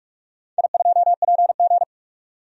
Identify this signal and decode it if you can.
Morse code, heard as a keyed tone, about 46 wpm, 700 Hz. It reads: I2PG
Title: DXCC-100 Call Signs - 4 Characters - 45wpm